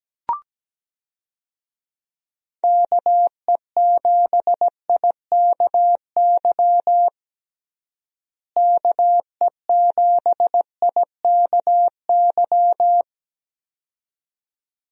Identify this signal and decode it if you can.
Morse code, heard as a keyed tone, about 17 wpm, 700 Hz. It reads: KE7IKY KE7IKY
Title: USA Call Signs - 6 Characters - Repeated 2x - 17wpm